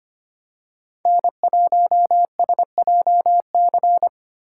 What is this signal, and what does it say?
Morse code, heard as a keyed tone, about 25 wpm, 700 Hz. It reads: N1SJC
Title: USA Call Signs - 5 Characters - 3x Character Spacing - 25wpm